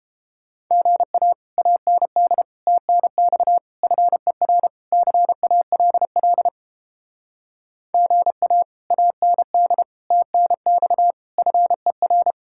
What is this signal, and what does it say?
Morse code, heard as a keyed tone, about 32 wpm, 700 Hz. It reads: GA AND TNX FER CALL GA AND TNX FER